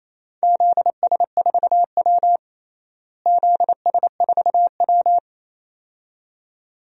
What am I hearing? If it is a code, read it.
Morse code, heard as a keyed tone, about 27 wpm, 700 Hz. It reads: ZS4W ZS4W
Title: DXCC-100 Call Signs - 4 Characters - Repeated 2x - 28wpm